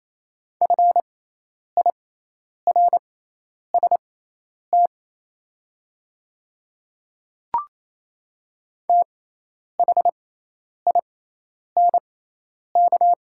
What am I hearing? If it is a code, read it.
Morse code, heard as a keyed tone, about 27 wpm, spaced as for 10 wpm, 700 Hz. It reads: FIRST THINK
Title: 6x Character Spacing - Top 300 Words - 28wpm